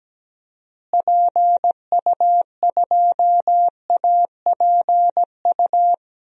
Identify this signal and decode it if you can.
Morse code, heard as a keyed tone, about 17 wpm, 700 Hz. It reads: PU2APU
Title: DXCC-100 Call Signs - 6 Characters - Repeated 2x - 17wpm